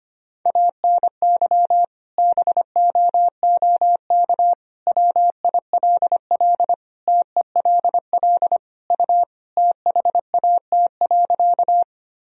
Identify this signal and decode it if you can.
Morse code, heard as a keyed tone, about 25 wpm, 700 Hz. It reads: ANY BOOK WILL TELL U THAT.
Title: Sentences from Top 300 Words and Abbreviations 25wpm